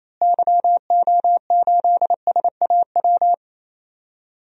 Morse code, heard as a keyed tone, about 28 wpm, 700 Hz. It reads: YO8SAW
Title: DXCC-100 Call Signs - 6 Characters - Repeated 3x - 28wpm